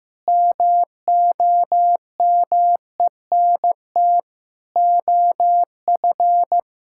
Morse code, heard as a keyed tone, about 15 wpm, 700 Hz. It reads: MOMENT OF
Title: Sentences from Top 1600 Words 15wpm